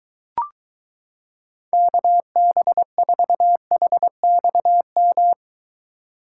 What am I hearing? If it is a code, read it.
Morse code, heard as a keyed tone, about 23 wpm, 700 Hz. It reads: KB4HXM